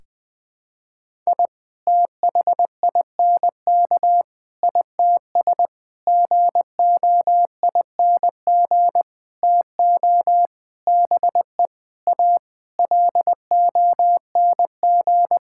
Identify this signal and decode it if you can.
Morse code, heard as a keyed tone, about 20 wpm, 700 Hz. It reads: I THINK ITS GOING TO BE A LONG
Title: Sentences with 7-Word Phrases - 20wpm